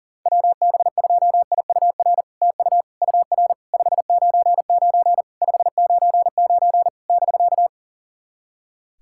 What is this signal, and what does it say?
Morse code, heard as a keyed tone, about 40 wpm, 700 Hz. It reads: WB2IUR TU UR 599 599 <BK>
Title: SOTA QSO Practice 40wpm